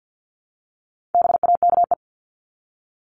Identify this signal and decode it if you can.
Morse code, heard as a keyed tone, about 50 wpm, 700 Hz. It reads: BAKE
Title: Top 1751-2000 Words 50wpm